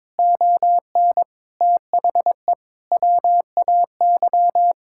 Morse code, heard as a keyed tone, about 21 wpm, 700 Hz. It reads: ON THE WAY
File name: Sets of 9 Words - 22wpm